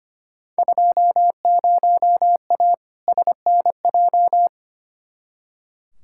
Morse code, heard as a keyed tone, about 25 wpm, 700 Hz. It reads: 20A SNJ
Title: Field Day Exchanges Encore - 25wpm